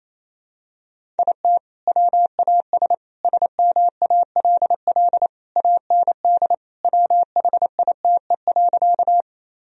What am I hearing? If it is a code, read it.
Morse code, heard as a keyed tone, about 27 wpm, 700 Hz. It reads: IT WAS SMALL AND WHITE.
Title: Sentences from Top 1000 Words - Review Part 1 - 28wpm